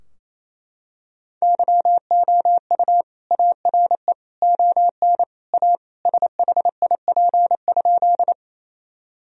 Morse code, heard as a keyed tone, about 28 wpm, 700 Hz. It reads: YOU ARE ON A SHIP?
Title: Sentences from Top 1000 Words - Review Part 2 - 28wpm